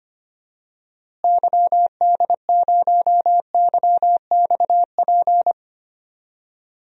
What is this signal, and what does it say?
Morse code, heard as a keyed tone, about 25 wpm, 700 Hz. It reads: YD0YXP
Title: DXCC-100 Call Signs - 6 Characters - Repeated 2x - 25wpm